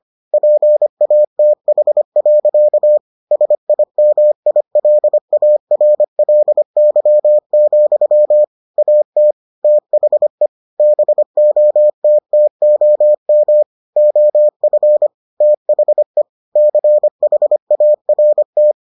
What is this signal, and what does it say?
Morse code, heard as a keyed tone, about 25 wpm, 600 Hz. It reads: PATH. SIMILARLY, AT THE BOTTOM OF THE CHART